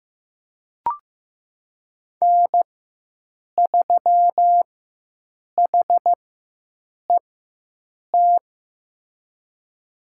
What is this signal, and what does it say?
Morse code, heard as a keyed tone, about 15 wpm, spaced as for 7 wpm, 700 Hz. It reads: N3HET
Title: USA Call Signs - 5 Characters - 4x Character Spacing - 15wpm